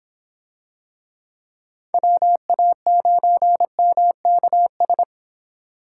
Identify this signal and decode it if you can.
Morse code, heard as a keyed tone, about 23 wpm, 700 Hz. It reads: WA9MKS